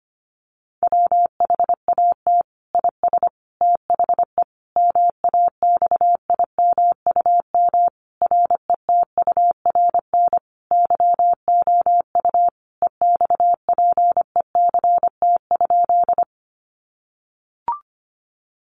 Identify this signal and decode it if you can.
Morse code, heard as a keyed tone, about 25 wpm, 700 Hz. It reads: WHAT IS THE MAXIMUM RETURN YOU EXPECT?
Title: Sentences from Top 1300 Words 25wpm